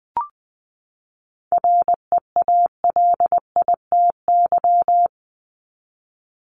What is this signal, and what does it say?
Morse code, heard as a keyed tone, about 20 wpm, 700 Hz. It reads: REALITY
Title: Top 1001-1250 Words 20wpm